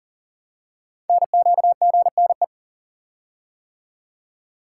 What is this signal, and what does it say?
Morse code, heard as a keyed tone, about 40 wpm, 700 Hz. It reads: NQGNE
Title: Code Groups - 5 Letters - 40wpm